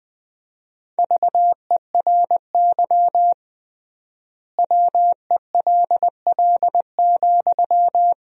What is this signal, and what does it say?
Morse code, heard as a keyed tone, about 20 wpm, 700 Hz. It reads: VERY WELL,
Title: 3x Word-Spacing - Sentences from Top 200 Words - 20wpm